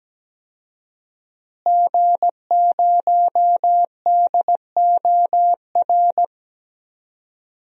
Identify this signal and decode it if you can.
Morse code, heard as a keyed tone, about 17 wpm, 700 Hz. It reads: G0DOR
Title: DXCC-100 Call Signs - 5 Characters - Repeated 2x - 17wpm